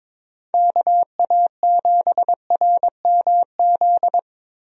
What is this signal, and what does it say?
Morse code, heard as a keyed tone, about 22 wpm, 700 Hz. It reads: KA7RMZ
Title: USA Call Signs - 6 Characters - Repeated 2x - 22wpm